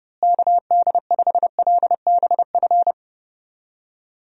Morse code, heard as a keyed tone, about 30 wpm, 700 Hz. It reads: KD5LBF
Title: USA Call Signs - 6 Characters - Repeated 3x - 30wpm